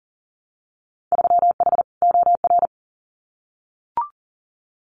Morse code, heard as a keyed tone, about 38 wpm, 700 Hz. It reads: 3H OR
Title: Winter Field Day Exchanges Encore 40wpm